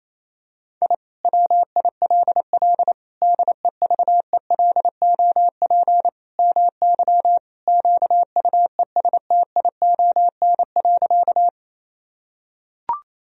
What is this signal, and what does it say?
Morse code, heard as a keyed tone, about 28 wpm, 700 Hz. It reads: I WILL DEVELOP MY QUESTION.
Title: Sentences from Top 1000 Words - Review Part 2 - 28wpm